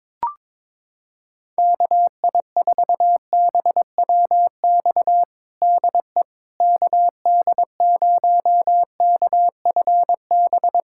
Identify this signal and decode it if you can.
Morse code, heard as a keyed tone, about 22 wpm, 700 Hz. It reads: KI4BWX DE KD0KFB
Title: USA Callsign DE Callsign - 6 Characters - 22wpm